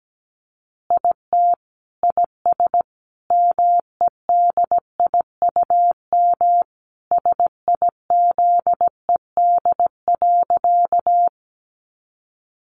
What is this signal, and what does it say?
Morse code, heard as a keyed tone, about 17 wpm, 700 Hz. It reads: IT IS MEDIUM SIZED.
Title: Easy Phrases 17wpm